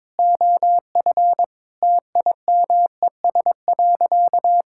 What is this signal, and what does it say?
Morse code, heard as a keyed tone, about 22 wpm, 700 Hz. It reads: OF TIMES.
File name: practice / 22wpm / Sentences with 8-Word Phrases - 22wpm